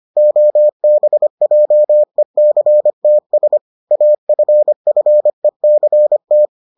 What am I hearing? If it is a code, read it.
Morse code, heard as a keyed tone, about 25 wpm, 600 Hz. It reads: OBJECTS AFFECT